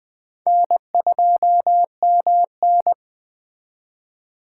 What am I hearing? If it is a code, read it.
Morse code, heard as a keyed tone, about 20 wpm, 700 Hz. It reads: N2MN